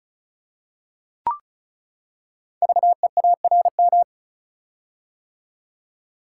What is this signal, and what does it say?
Morse code, heard as a keyed tone, about 36 wpm, 690 Hz. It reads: VEARM
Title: Code Groups - 5 Letters - 35wpm